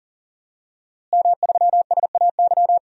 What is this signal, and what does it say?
Morse code, heard as a keyed tone, about 40 wpm, 700 Hz. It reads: M3SAY
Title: DXCC-100 Call Signs - 5 Characters - Repeated 3x - 40wpm